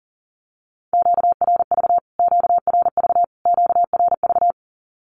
Morse code, heard as a keyed tone, about 40 wpm, 700 Hz. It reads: QRV QRV QRV
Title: Q-Codes Repeated 3x 40wpm